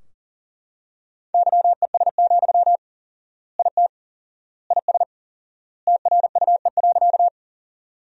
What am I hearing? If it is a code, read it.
Morse code, heard as a keyed tone, about 40 wpm, 700 Hz. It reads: YES, IT IS TRUE.
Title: 4x Word-Spacing - Sentences from Top 500 Words - 40wpm